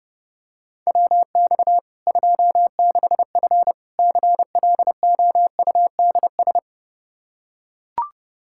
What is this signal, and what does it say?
Morse code, heard as a keyed tone, about 30 wpm, 700 Hz. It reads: WX 26F CLOUDS
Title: Practice QSO - Exchange 2 - Info - 30wpm